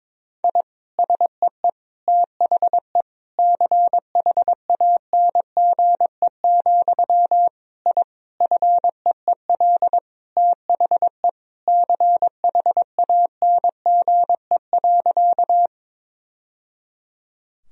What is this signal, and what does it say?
Morse code, heard as a keyed tone, about 22 wpm, 700 Hz. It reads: I SEE THE CHANGE, I FEEL THE CHANGE.
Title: Sentences from Top 1000 Words - Review Part 1 - 22wpm